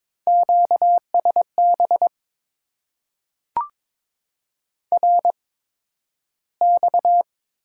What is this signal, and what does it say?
Morse code, heard as a keyed tone, about 22 wpm, spaced as for 12 wpm, 700 Hz. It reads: QSB RX
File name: Top 90 QSO Elements - 8x Character Spacing - 22wpm